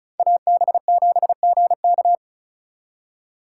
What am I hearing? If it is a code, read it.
Morse code, heard as a keyed tone, about 42 wpm, 700 Hz. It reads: AB7GK
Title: Speed Racing - USA 2x2 Call Signs 35wpm